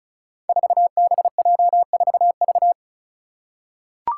Morse code, heard as a keyed tone, about 36 wpm, 700 Hz. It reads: 4BJ4V
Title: Five Challenging Letter-Number Combinations 35wpm